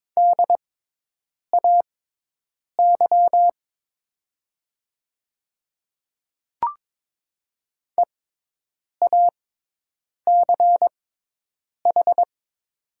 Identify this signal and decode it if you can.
Morse code, heard as a keyed tone, about 22 wpm, spaced as for 8 wpm, 700 Hz. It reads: DAY EACH
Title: 6x Character Spacing - Top 100 Words - 22wpm